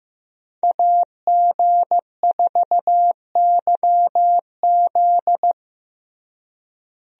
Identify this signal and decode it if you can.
Morse code, heard as a keyed tone, about 15 wpm, 700 Hz. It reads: AG4YZ
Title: USA Call Signs - 5 Characters - Repeated 2x - 15wpm